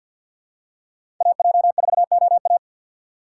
Morse code, heard as a keyed tone, about 51 wpm, 700 Hz. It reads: AJ4OA
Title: USA Call Signs - 5 Characters - 2x Character Spacing - 50wpm